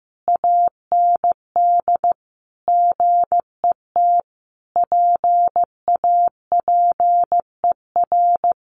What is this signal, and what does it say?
Morse code, heard as a keyed tone, about 15 wpm, 700 Hz. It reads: AND GET PAPER
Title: Sentences from Top 1500 Words 15wpm